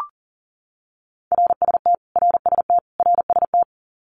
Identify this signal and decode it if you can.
Morse code, heard as a keyed tone, about 40 wpm, 700 Hz. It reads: RST RST RST
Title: Abbreviations Repeated 3x - 40wpm